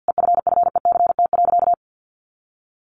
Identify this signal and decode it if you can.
Morse code, heard as a keyed tone, about 50 wpm, 700 Hz. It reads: EFFECT.
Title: Sentences from Top 1400 Words 50wpm